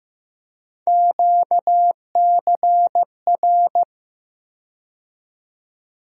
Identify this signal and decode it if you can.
Morse code, heard as a keyed tone, about 15 wpm, 700 Hz. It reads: QCR
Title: USA Call Sign Postfixes - 3 Letters 15wpm